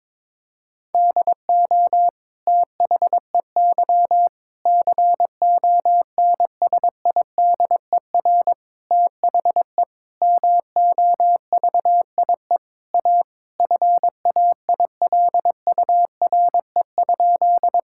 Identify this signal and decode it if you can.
Morse code, heard as a keyed tone, about 22 wpm, 700 Hz. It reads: DO THEY CONSIDER THE MOVIE A FAILURE?
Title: Sentences from Top 1000 Words - Review Part 6 - 22wpm